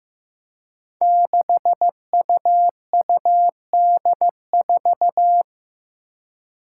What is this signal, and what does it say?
Morse code, heard as a keyed tone, about 15 wpm, 700 Hz. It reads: 6UUD4
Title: Five Challenging Letter-Number Combinations 15wpm